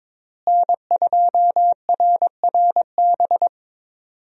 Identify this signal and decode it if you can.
Morse code, heard as a keyed tone, about 21 wpm, 700 Hz. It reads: N2RRB
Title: USA Call Signs - 5 Characters - Repeated 3x - 22wpm